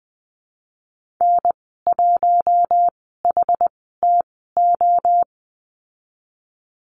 Morse code, heard as a keyed tone, about 20 wpm, spaced as for 14 wpm, 700 Hz. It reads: N1HTO
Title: 069 Lesson 12g - Letter d - Call Signs 20wpm